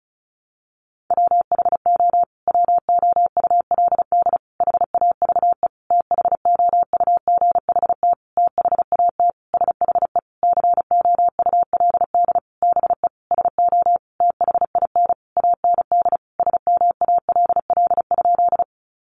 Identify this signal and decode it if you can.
Morse code, heard as a keyed tone, about 34 wpm, 700 Hz. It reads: WHO WOULD HAVE THOUGHT THAT SHE COULD BE SO THIN AND SMALL?
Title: Sentences from Top 1200 Words 35wpm